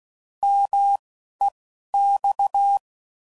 Morse code, heard as a keyed tone, about 16 wpm, 800 Hz. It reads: M E X
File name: Mixed / Contrasts / 3LtrContrasts